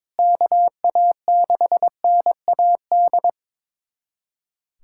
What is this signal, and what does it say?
Morse code, heard as a keyed tone, about 22 wpm, 700 Hz. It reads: KA6NAD
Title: USA Callsign DE Callsign - 6 Characters - 22wpm